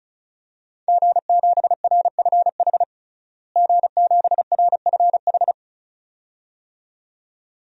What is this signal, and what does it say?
Morse code, heard as a keyed tone, about 36 wpm, 700 Hz. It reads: G7RFH G7RFH
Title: DXCC-100 Call Signs - 5 Characters - Repeated 2x - 35wpm